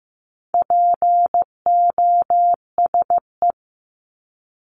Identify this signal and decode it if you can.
Morse code, heard as a keyed tone, about 15 wpm, 700 Hz. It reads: POSE
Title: Top 500 Words 15wpm